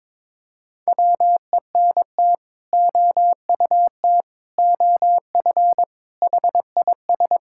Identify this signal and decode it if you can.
Morse code, heard as a keyed tone, about 22 wpm, 700 Hz. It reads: WENT OUT OF HIS
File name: practice / 22wpm / Sets of 6 Words - 22wpm